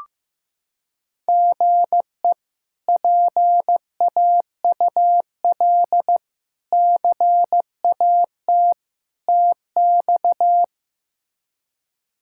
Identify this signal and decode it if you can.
Morse code, heard as a keyed tone, about 15 wpm, 700 Hz. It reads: GE PAUL CAT TX
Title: K1USN SST QSO Practice - 15wpm